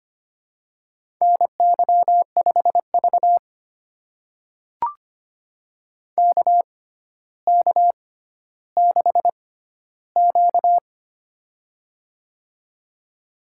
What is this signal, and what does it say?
Morse code, heard as a keyed tone, about 25 wpm, spaced as for 13 wpm, 700 Hz. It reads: NY5V KK6Q
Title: USA Call Signs - 4 Characters - 6x Character Spacing - 25wpm